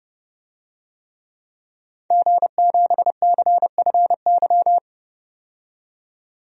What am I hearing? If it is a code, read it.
Morse code, heard as a keyed tone, about 32 wpm, 700 Hz. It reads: G7CFY
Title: DXCC-100 Call Signs - 5 Characters - 30wpm